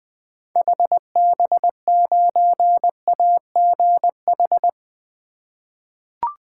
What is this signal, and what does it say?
Morse code, heard as a keyed tone, about 20 wpm, 700 Hz. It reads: HB9AGH
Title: 2023 SOTA Chaser Call Signs - 20wpm